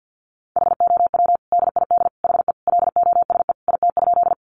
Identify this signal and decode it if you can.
Morse code, heard as a keyed tone, about 50 wpm, 700 Hz. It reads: HOW DID HE LOSE IT?
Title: Sentences from Top 1100 Words 50wpm